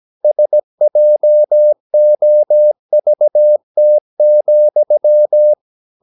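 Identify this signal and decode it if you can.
Morse code, heard as a keyed tone, about 17 wpm, 600 Hz. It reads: SJOVT,
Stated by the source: Slow (QRS) Morse practice at 17 wpm, from CW med Gnister.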